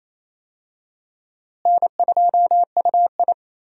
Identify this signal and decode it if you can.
Morse code, heard as a keyed tone, about 28 wpm, 700 Hz. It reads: N2UI
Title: USA Call Signs - 4 Characters - Repeated 3x - 28wpm